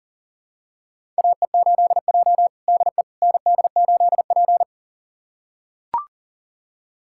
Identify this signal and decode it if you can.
Morse code, heard as a keyed tone, about 40 wpm, 700 Hz. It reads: AE8J DE ND8P
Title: USA Callsign DE Callsign - 4 Characters - 40wpm